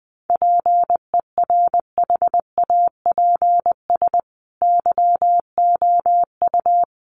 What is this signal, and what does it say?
Morse code, heard as a keyed tone, about 20 wpm, 700 Hz. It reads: PERHAPS YOU
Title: Sentences from Top 1400 Words 20wpm